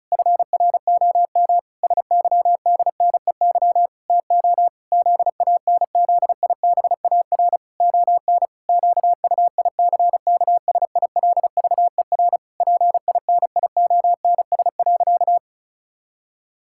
Morse code, heard as a keyed tone, about 36 wpm, 700 Hz. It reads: FROM SYDNEY TO ZANZIBAR ON QUICKSILVER PINIONS.
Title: Pangram Challenge v2 - 35wpm